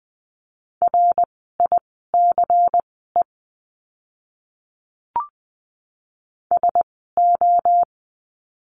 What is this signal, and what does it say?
Morse code, heard as a keyed tone, about 20 wpm, spaced as for 14 wpm, 700 Hz. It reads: RICE SO
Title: 094 Lesson 16f - Letter c - Words 20wpm